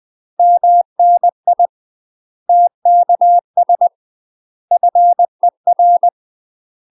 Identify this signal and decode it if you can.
Morse code, heard as a keyed tone, about 20 wpm, 700 Hz. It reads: MNI TKS FER
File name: UK MASTER QSOs TRAINING / 20WPM / UK MASTER QSOs 0045